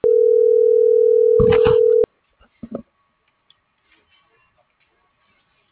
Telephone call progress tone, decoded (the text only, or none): ringback tone